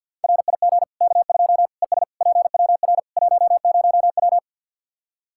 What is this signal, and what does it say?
Morse code, heard as a keyed tone, about 49 wpm, 700 Hz. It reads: RIG K2 ES PWR 10W
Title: Practice QSO - Exchange 2 - Info - 50wpm